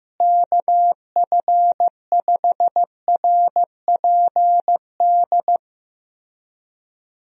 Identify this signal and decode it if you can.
Morse code, heard as a keyed tone, about 15 wpm, 700 Hz. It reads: KF5RPD
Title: USA Call Signs - 6 Characters - Repeated 3x - 15wpm